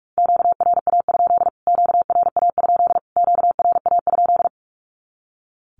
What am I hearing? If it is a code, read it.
Morse code, heard as a keyed tone, about 44 wpm, 700 Hz. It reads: QRA? QRA? QRA?
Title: Q-Codes Repeated 3x 45wpm